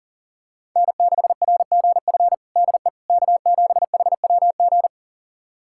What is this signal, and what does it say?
Morse code, heard as a keyed tone, about 40 wpm, 700 Hz. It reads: N6RGF DE K7HWG
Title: USA Callsign DE Callsign - 5 Characters - 40wpm